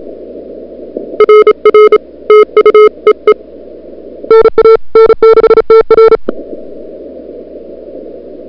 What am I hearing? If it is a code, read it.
Morse code, heard as a keyed tone, about 33 wpm, 440 Hz. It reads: RR TU EE NA N6TR
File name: n6tr / Fab2007CWSprintHour1